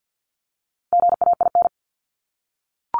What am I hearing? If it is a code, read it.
Morse code, heard as a keyed tone, about 50 wpm, 700 Hz. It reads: GAIN
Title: Top 751-1000 Words 50wpm